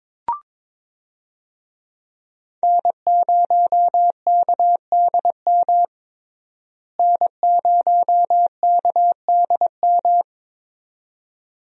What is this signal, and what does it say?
Morse code, heard as a keyed tone, about 22 wpm, 700 Hz. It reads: N0KDM N0KDM
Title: USA Call Signs - 5 Characters - Repeated 2x - 22wpm